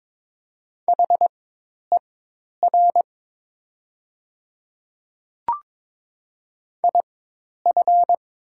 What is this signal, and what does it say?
Morse code, heard as a keyed tone, about 22 wpm, spaced as for 10 wpm, 700 Hz. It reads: HER IF
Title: 4x Character Spacing - Top 200 Words - 22wpm